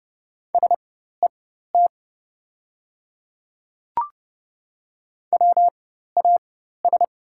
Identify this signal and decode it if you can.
Morse code, heard as a keyed tone, about 30 wpm, spaced as for 14 wpm, 700 Hz. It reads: SET WAS
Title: 4x Character Spacing - Top 200 Words - 30wpm